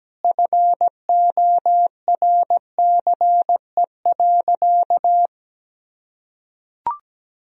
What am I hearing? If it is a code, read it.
Morse code, heard as a keyed tone, about 17 wpm, 700 Hz. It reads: FORCE.
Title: Sentences from Top 1000 Words - Review Part 3 - 17wpm